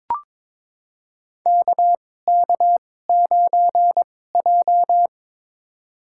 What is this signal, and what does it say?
Morse code, heard as a keyed tone, about 22 wpm, spaced as for 16 wpm, 700 Hz. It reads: KK9J
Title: USA Call Signs - 4 Characters - 2x Character Spacing - 22wpm